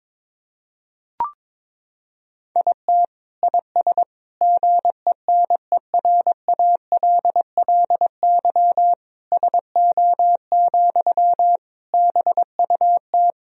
Morse code, heard as a keyed tone, about 22 wpm, 700 Hz. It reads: IT IS GENERALLY SO, BUT
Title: Sentences from Top 1000 Words - Review Part 4 - 22wpm